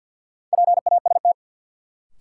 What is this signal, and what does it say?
Morse code, heard as a keyed tone, about 52 wpm, 700 Hz. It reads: PAST
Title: Sentences from Top 600 Words - New Word Drill - 50wpm